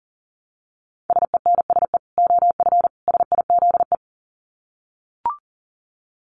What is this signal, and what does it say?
Morse code, heard as a keyed tone, about 40 wpm, 700 Hz. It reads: SENSE OF SIZE
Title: Dit Challenge 40wpm